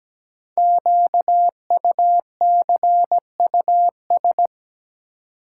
Morse code, heard as a keyed tone, about 17 wpm, 700 Hz. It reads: QUCUS
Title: Code Groups - 5 Letters - 17wpm